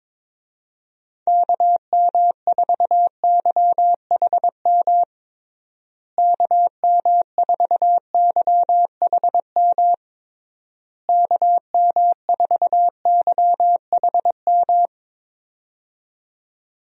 Morse code, heard as a keyed tone, about 22 wpm, 700 Hz. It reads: KM4YHM KM4YHM KM4YHM
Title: USA Call Signs - 6 Characters - Repeated 3x - 22wpm